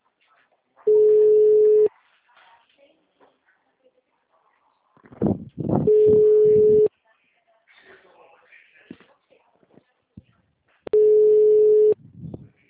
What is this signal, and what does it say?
Ringback tone (Europe, 425 Hz), with speech.